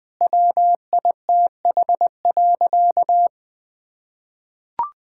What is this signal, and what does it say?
Morse code, heard as a keyed tone, about 20 wpm, 700 Hz. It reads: WITH.
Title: Sentences from Top 1000 Words - Review Part 5 - 20wpm